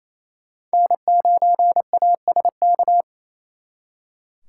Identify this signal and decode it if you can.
Morse code, heard as a keyed tone, about 37 wpm, 700 Hz. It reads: N9ASK